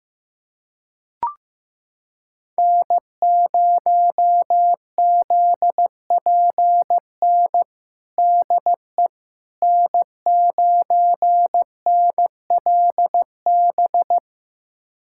Morse code, heard as a keyed tone, about 15 wpm, 700 Hz. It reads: N0ZPN DE N9NLB
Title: USA Callsign DE Callsign 15wpm